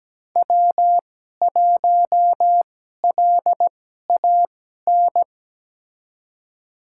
Morse code, heard as a keyed tone, about 17 wpm, spaced as for 12 wpm, 700 Hz. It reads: W1LAN
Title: USA Call Signs - 5 Characters - 2x Character Spacing - 17wpm